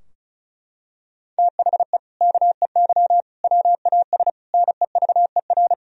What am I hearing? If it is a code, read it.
Morse code, heard as a keyed tone, about 34 wpm, 700 Hz. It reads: THE KEY WAS NEVER
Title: Sentences from Top 1000 Words - Review Part 4 - 35wpm